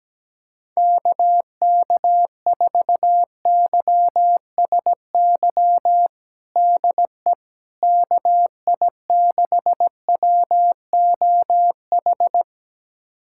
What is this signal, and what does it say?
Morse code, heard as a keyed tone, about 17 wpm, 700 Hz. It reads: KK4YSY DE KI6WOH
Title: USA Callsign DE Callsign 17wpm